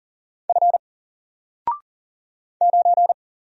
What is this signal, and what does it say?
Morse code, heard as a keyed tone, about 39 wpm, 700 Hz. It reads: F 9
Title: Single Character (Letter:Number:Punctuation:Prosign) 40wpm